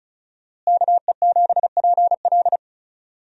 Morse code, heard as a keyed tone, about 34 wpm, 700 Hz. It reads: KE7PL
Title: USA Call Signs - 5 Characters - Repeated 2x - 35wpm